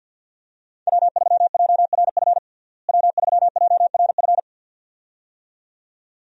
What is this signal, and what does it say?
Morse code, heard as a keyed tone, about 48 wpm, 700 Hz. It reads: W3JRF W3JRF
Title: USA Call Signs - 5 Characters - Repeated 2x - 50wpm